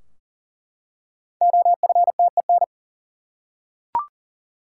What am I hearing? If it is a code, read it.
Morse code, heard as a keyed tone, about 40 wpm, 700 Hz. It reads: OFTEN
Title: Five Letter Words - 40wpm